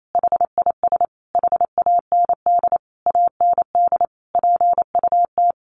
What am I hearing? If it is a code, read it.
Morse code, heard as a keyed tone, about 28 wpm, 700 Hz. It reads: HIS HAND AND PUT
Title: Sentences from Top 1300 Words 28wpm